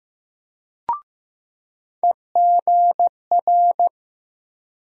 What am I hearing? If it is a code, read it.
Morse code, heard as a keyed tone, about 15 wpm, 700 Hz. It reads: EGR
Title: USA Call Sign Postfixes - 3 Letters 15wpm